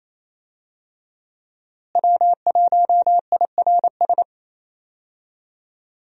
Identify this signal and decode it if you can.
Morse code, heard as a keyed tone, about 27 wpm, 700 Hz. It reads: W1IRS
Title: USA Call Signs - 5 Characters - Repeated 2x - 28wpm